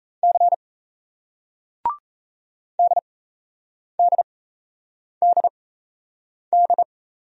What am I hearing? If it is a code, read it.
Morse code, heard as a keyed tone, about 38 wpm, spaced as for 8 wpm, 700 Hz. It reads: C DDDD